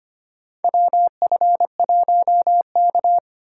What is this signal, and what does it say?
Morse code, heard as a keyed tone, about 25 wpm, 700 Hz. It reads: WF1K
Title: USA Call Signs - 4 Characters - 4x Character Spacing - 25wpm